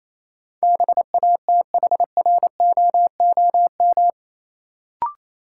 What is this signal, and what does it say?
Morse code, heard as a keyed tone, about 28 wpm, 700 Hz. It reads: BATHROOM
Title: Sets of 5 Words Encore - New Word Drill - 28wpm